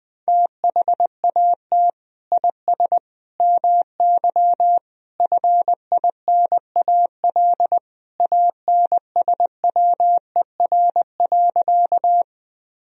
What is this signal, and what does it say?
Morse code, heard as a keyed tone, about 20 wpm, 700 Hz. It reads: THAT IS MY FINAL ANSWER.
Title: Sentences from Top 1000 Words - Review Part 2 - 20wpm